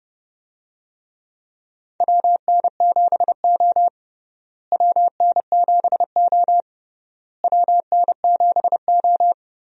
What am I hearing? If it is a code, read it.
Morse code, heard as a keyed tone, about 31 wpm, 700 Hz. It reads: WN7O WN7O WN7O
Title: USA Call Signs - 4 Characters - Repeated 3x - 30wpm